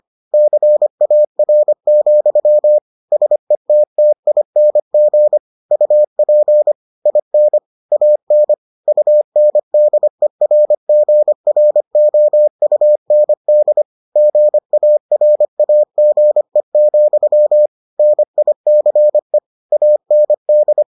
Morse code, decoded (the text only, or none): CAR, SETTING UP IN AN UNDERGROUND GARAGE, NICE AND